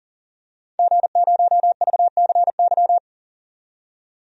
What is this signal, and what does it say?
Morse code, heard as a keyed tone, about 40 wpm, 700 Hz. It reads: G0VCY
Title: DXCC-100 Call Signs - 5 Characters - 40wpm